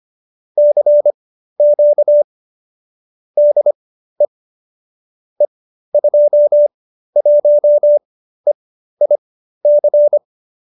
Morse code, heard as a keyed tone, about 25 wpm, spaced as for 13 wpm, 570 Hz. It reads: CQ DE E21EIC
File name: qso208b 13